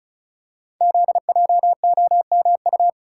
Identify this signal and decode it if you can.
Morse code, heard as a keyed tone, about 36 wpm, 700 Hz. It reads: ZJOMU
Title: Code Groups - 5 Letters - 35wpm